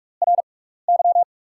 Morse code, heard as a keyed tone, about 48 wpm, spaced as for 16 wpm, 690 Hz. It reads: RY
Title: 6x Character Spacing - Top 200 Words - 45wpm